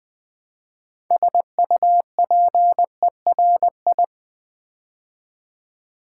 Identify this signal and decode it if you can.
Morse code, heard as a keyed tone, about 20 wpm, 700 Hz. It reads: SUPERI
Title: Build A Word - 3 Letters At A Time - 8-Letter Words - 20wpm